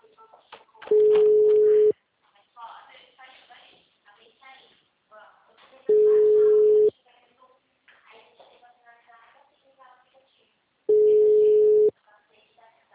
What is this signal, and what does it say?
Ringback tone (Europe, 425 Hz), with speech.